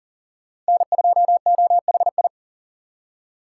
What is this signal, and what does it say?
Morse code, heard as a keyed tone, about 51 wpm, 700 Hz. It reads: N2OHI